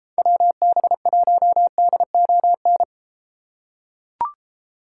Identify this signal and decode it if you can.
Morse code, heard as a keyed tone, about 30 wpm, 700 Hz. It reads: WB1DON